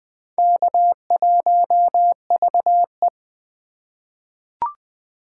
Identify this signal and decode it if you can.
Morse code, heard as a keyed tone, about 21 wpm, 700 Hz. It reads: K1VE
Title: USA Call Signs - 4 Characters - 4x Character Spacing - 20wpm